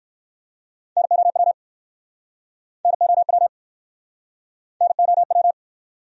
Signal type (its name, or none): Morse code